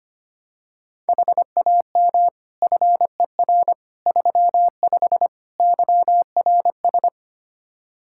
Morse code, heard as a keyed tone, about 25 wpm, 700 Hz. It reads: HAM FER 35 YRS
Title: Practice QSO - Exchange 3 - More Info - 25wpm